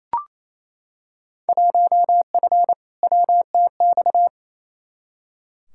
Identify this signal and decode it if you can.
Morse code, heard as a keyed tone, about 28 wpm, 700 Hz. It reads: 1F WTX
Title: Field Day Exchanges Encore - 28wpm